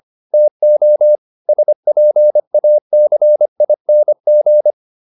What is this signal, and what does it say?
Morse code, heard as a keyed tone, about 25 wpm, 600 Hz. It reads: TO SPACING